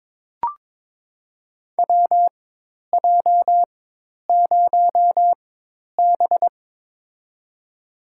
Morse code, heard as a keyed tone, about 22 wpm, spaced as for 10 wpm, 700 Hz. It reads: WJ0B
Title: USA Call Signs - 4x Character Spacing - 22wpm